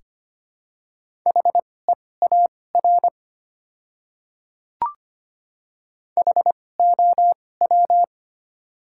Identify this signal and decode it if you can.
Morse code, heard as a keyed tone, about 25 wpm, spaced as for 18 wpm, 700 Hz. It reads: HEAR HOW
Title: 2x Character Spacing - Top 300 Words - 25wpm